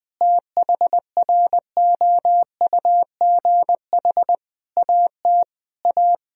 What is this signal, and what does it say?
Morse code, heard as a keyed tone, about 20 wpm, 700 Hz. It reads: THROUGH AT A
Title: Sentences from Top 1000 Words - Review Part 1 - 20wpm